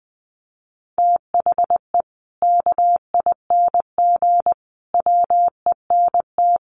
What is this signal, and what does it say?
Morse code, heard as a keyed tone, about 20 wpm, 700 Hz. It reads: THE KING WENT
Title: Sentences from Top 400 Words 20wpm